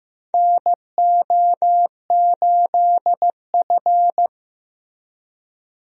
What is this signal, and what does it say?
Morse code, heard as a keyed tone, about 15 wpm, 700 Hz. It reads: NO8F
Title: USA Call Signs - 4 Characters - Repeated 3x - 15wpm